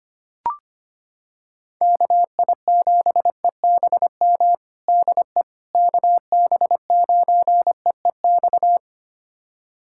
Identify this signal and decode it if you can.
Morse code, heard as a keyed tone, about 25 wpm, 700 Hz. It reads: KI7EBM DE KB9EEX
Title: USA Callsign DE Callsign - 6 Characters - 25wpm